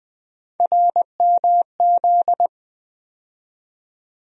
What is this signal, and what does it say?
Morse code, heard as a keyed tone, about 21 wpm, 700 Hz. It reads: RMZ
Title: USA Call Sign Postfixes - 3 Letters 20wpm